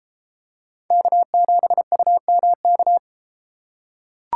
Morse code, heard as a keyed tone, about 32 wpm, 700 Hz. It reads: K7UMK